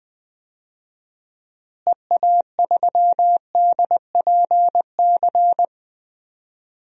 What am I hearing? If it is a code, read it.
Morse code, heard as a keyed tone, about 21 wpm, 700 Hz. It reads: EA3DPC
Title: DXCC-100 Call Signs - 6 Characters - Repeated 3x - 20wpm